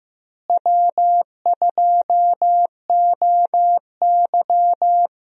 Morse code, heard as a keyed tone, about 15 wpm, 700 Hz. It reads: W2OY
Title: USA Call Signs - 4 Characters - Repeated 3x - 15wpm